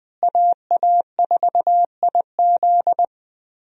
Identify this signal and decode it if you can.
Morse code, heard as a keyed tone, about 20 wpm, 700 Hz. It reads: AA4IZ
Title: USA Call Signs - 5 Characters - Repeated 3x - 20wpm